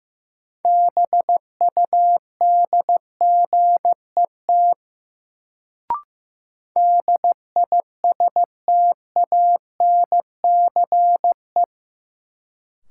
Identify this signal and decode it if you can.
Morse code, heard as a keyed tone, about 15 wpm, 700 Hz. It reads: BUDGET DISTANCE
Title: Sentences from Top 1200 Words - New Word Drill Alt - 15wpm